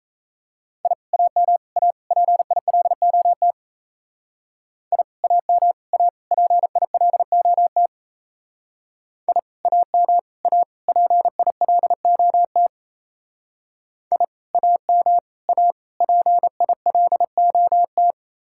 Morse code, heard as a keyed tone, about 35 wpm, 700 Hz. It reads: I AM A PILOT I AM A PILOT I AM A PILOT I AM A PILOT